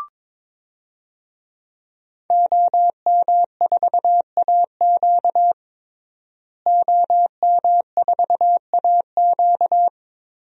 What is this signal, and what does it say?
Morse code, heard as a keyed tone, about 22 wpm, 700 Hz. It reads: OM4AQ OM4AQ
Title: DXCC-100 Call Signs - 5 Characters - Repeated 3x - 22wpm